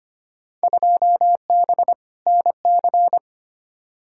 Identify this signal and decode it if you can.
Morse code, heard as a keyed tone, about 25 wpm, 700 Hz. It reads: 2B NC
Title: Field Day Exchanges Encore - 25wpm